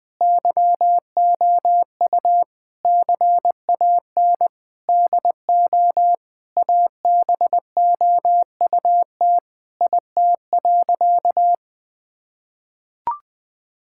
Morse code, heard as a keyed tone, about 20 wpm, 700 Hz. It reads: YOU CAN DO ABOUT IT.
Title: Sentences with 8-Word Phrases - Spoken Beginning - 20wpm